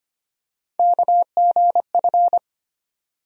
Morse code, heard as a keyed tone, about 25 wpm, 700 Hz. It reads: KGF
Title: USA Call Sign Postfixes - 3 Letters 25wpm